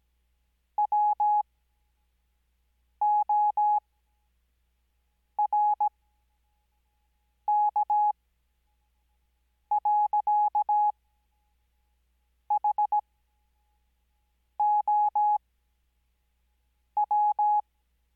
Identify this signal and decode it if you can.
Morse code, heard as a keyed tone, about 17 wpm, spaced as for 5 wpm, 840 Hz. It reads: WORK. HOW
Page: Farnsworth Spacing: 15WPM Character speed Spaced at 5WPM.